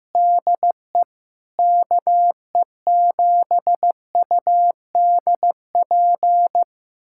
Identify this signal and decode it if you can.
Morse code, heard as a keyed tone, about 15 wpm, 700 Hz. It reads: DE KE7UDP
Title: USA Callsign DE Callsign - 6 Characters - 15wpm